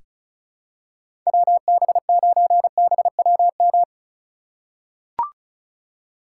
Morse code, heard as a keyed tone, about 36 wpm, 700 Hz. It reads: WB9BWM
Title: USA Call Signs - 4x Character Spacing - 35wpm